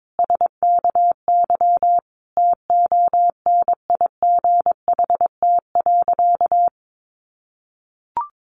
Morse code, heard as a keyed tone, about 22 wpm, 700 Hz. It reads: SKY TONIGHT.
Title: Sentences from Top 1300 Words 22wpm